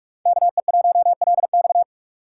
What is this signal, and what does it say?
Morse code, heard as a keyed tone, about 46 wpm, 700 Hz. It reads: KE1LX
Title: USA Call Signs - 3x Character Spacing - 45wpm